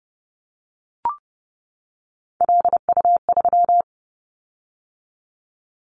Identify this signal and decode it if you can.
Morse code, heard as a keyed tone, about 30 wpm, 700 Hz. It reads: LU3
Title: Callsign Prefixes for Top 100 DXCC - 30wpm